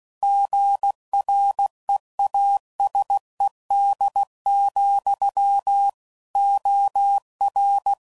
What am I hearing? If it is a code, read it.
Morse code, heard as a keyed tone, about 16 wpm, 800 Hz. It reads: GREASED, OR
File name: Sayings / Chuckles / Squeaky